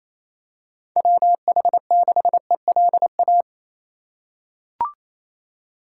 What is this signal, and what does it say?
Morse code, heard as a keyed tone, about 28 wpm, 700 Hz. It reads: WH6ELA
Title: USA Call Signs - 6 Characters - 3x Character Spacing - 28wpm